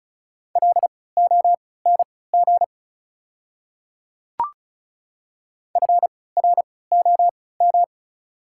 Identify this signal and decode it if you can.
Morse code, heard as a keyed tone, about 36 wpm, spaced as for 20 wpm, 690 Hz. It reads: LONG FROM
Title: 3x Character Spacing - Top 100 Words - 35wpm